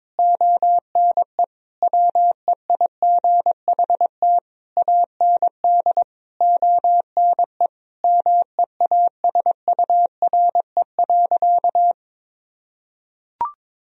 Morse code, heard as a keyed tone, about 22 wpm, 700 Hz. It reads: ONE WEIGHT AND ONE MEASURE.
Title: Sentences from Top 1000 Words - Review Part 3 - 22wpm